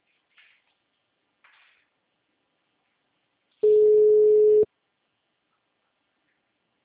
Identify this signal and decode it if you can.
Ringback tone (Europe, 425 Hz).